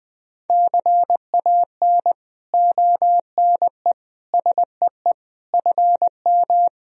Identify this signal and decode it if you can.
Morse code, heard as a keyed tone, about 20 wpm, 700 Hz. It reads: CAN ONE SEE FM
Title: Sentences from Top 100 Words and Abbreviations 20wpm